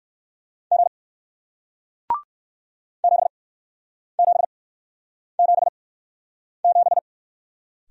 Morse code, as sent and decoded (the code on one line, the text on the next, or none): --. / --... --... --... --...
G 7777